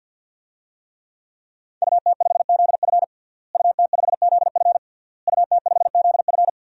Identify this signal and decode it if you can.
Morse code, heard as a keyed tone, about 52 wpm, 700 Hz. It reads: UT5ZF UT5ZF UT5ZF
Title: DXCC-100 Call Signs - 5 Characters - Repeated 3x - 50wpm